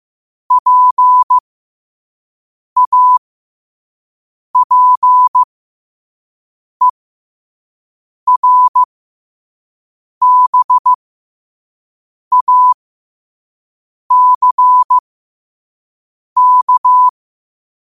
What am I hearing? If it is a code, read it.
Morse code, heard as a keyed tone, about 15 wpm, spaced as for 5 wpm, 1000 Hz. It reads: PAPERBACK
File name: qotdfiles / 5wpm